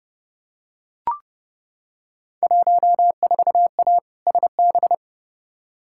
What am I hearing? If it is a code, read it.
Morse code, heard as a keyed tone, about 32 wpm, 700 Hz. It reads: 14A SB
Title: Field Day Exchanges Encore - 30wpm